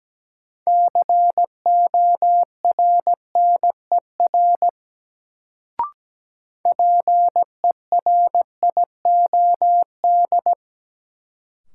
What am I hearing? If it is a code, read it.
Morse code, heard as a keyed tone, about 17 wpm, 700 Hz. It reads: CORNER PERIOD
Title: Sets of 5 Words - New Word Drill - 17wpm